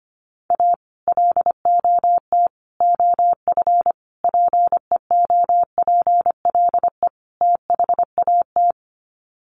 Morse code, heard as a keyed tone, about 25 wpm, 700 Hz. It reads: A LOT OF PEOPLE THAT
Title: Sets of 5 Words Encore2 25wpm